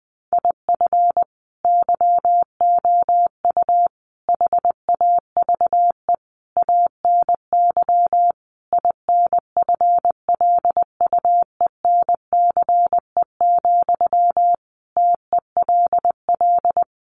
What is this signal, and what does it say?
Morse code, heard as a keyed tone, about 20 wpm, 700 Hz. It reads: IF YOU HAVE ANY INFLUENCE, TELL
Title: Sentences from Top 1100 Words 20wpm